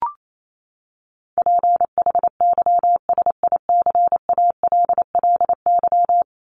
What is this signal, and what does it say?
Morse code, heard as a keyed tone, about 28 wpm, 700 Hz. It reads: PHYSICALLY
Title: Top 1501-1750 Words 28wpm